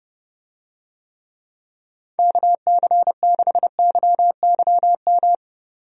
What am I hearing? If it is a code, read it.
Morse code, heard as a keyed tone, about 30 wpm, 700 Hz. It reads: KC6YYM
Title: USA Call Signs - 6 Characters - Repeated 2x - 30wpm